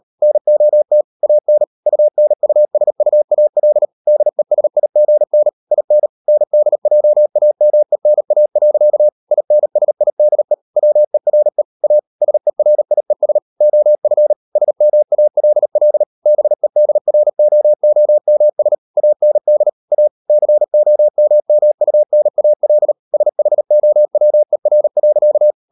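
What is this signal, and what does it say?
Morse code, heard as a keyed tone, about 38 wpm, 600 Hz. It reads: NOT AN UNUSUAL DESIGN IN NDJAMENA. INSIDE WERE A SERIES OF SMALL BEDROOMS AND A COMMUNAL SHOWER.
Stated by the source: Morse code practice at 38 WPM, sending Ken Follett's novel Never in English, from CW med Gnister.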